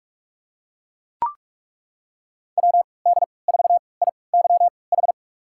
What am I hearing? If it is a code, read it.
Morse code, heard as a keyed tone, about 41 wpm, spaced as for 25 wpm, 700 Hz. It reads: WD4IYH
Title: USA Call Signs - 6 Characters - 3x Character Spacing - 45wpm